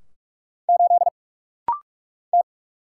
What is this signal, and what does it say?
Morse code, heard as a keyed tone, about 44 wpm, 700 Hz. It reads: 8 T
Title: Single Letter-Number - Rapid-Fire 45wpm